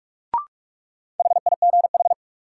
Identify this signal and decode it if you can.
Morse code, heard as a keyed tone, about 41 wpm, 700 Hz. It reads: HIGH
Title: Sentences from Top 200 Words - New Word Drill - 45wpm